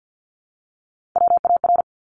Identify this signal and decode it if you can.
Morse code, heard as a keyed tone, about 49 wpm, 700 Hz. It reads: WAR
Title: Top 500 Words Repeated 50wpm